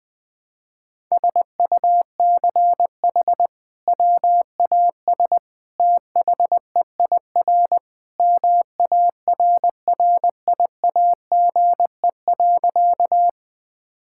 Morse code, heard as a keyed tone, about 19 wpm, 700 Hz. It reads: SUCH WAS THEIR MARRIAGE.
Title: Sentences from Top 1000 Words - Review Part 5 - 20wpm